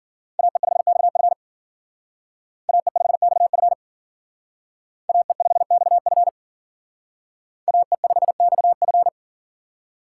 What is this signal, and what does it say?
Morse code, heard as a keyed tone, about 52 wpm, 700 Hz. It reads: AE5XF AE5XF AE5XF AE5XF